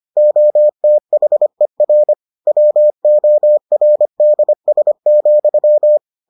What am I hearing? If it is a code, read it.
Morse code, heard as a keyed tone, about 25 wpm, 600 Hz. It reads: OTHER WORDS,